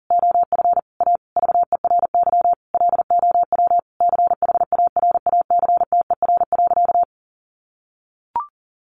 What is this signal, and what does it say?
Morse code, heard as a keyed tone, about 37 wpm, 700 Hz. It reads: OF A VERY LOW CHARACTER.
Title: Sentences from Top 1200 Words 40wpm